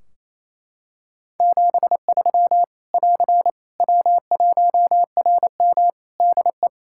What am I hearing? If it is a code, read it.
Morse code, heard as a keyed tone, about 27 wpm, 700 Hz. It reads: 73 <AR> W1RM DE
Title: Practice QSO - Exchange 4 - Ending - 28wpm